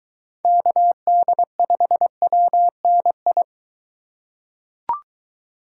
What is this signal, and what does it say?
Morse code, heard as a keyed tone, about 23 wpm, 700 Hz. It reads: KD5WNI